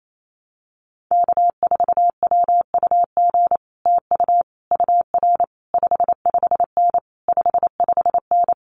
Morse code, heard as a keyed tone, about 28 wpm, 700 Hz. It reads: K4WUG TU UR 55N 55N
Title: Common QSO Phrases Encore 28wpm